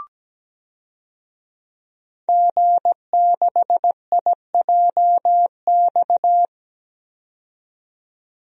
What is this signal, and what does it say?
Morse code, heard as a keyed tone, about 17 wpm, 700 Hz. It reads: G6IJX
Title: DXCC-100 Call Signs - 5 Characters - 17wpm